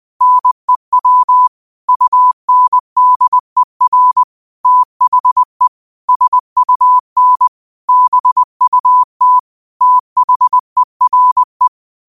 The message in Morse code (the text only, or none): NEW UNDER THE SUN BUT THERE